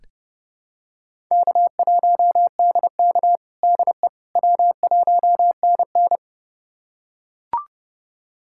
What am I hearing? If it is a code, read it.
Morse code, heard as a keyed tone, about 32 wpm, 700 Hz. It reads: K1DK DE W1NN
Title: USA Callsign DE Callsign - 4 Characters - 30wpm